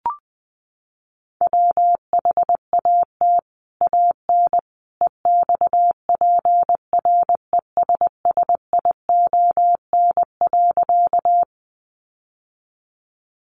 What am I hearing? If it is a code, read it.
Morse code, heard as a keyed tone, about 20 wpm, 700 Hz. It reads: WHAT AN EXPRESSION.
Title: Sentences from Top 1200 Words 20wpm